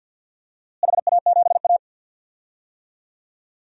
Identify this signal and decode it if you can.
Morse code, heard as a keyed tone, about 52 wpm, 700 Hz. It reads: HA7A
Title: DXCC-100 Call Signs - 4 Characters - Repeated 2x - 50wpm